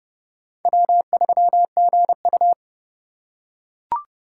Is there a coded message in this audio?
Yes: Morse code from 0.7 to 2.5 s, heard as a keyed tone, about 28 wpm, 700 Hz. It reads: W3GU